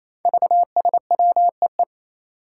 Morse code, heard as a keyed tone, about 28 wpm, 700 Hz. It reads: VSWEE
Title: Five Challenging Letter-Number Combinations 28wpm